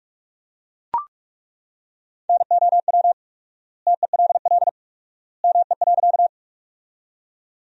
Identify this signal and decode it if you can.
Morse code, heard as a keyed tone, about 41 wpm, 700 Hz. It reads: NOW TELL ME.
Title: 4x Word-Spacing - Sentences from Top 200 Words - 45wpm